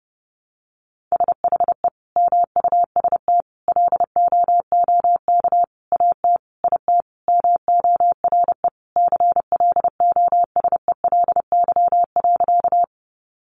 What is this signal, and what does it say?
Morse code, heard as a keyed tone, about 30 wpm, 700 Hz. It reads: SHE MUST LOOK AT IT MORE CLOSELY.
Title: Sentences from Top 1400 Words 30wpm